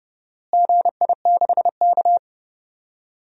Morse code, heard as a keyed tone, about 32 wpm, 700 Hz. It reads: GI6K
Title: DXCC-100 Call Signs - 4 Characters - Repeated 2x - 30wpm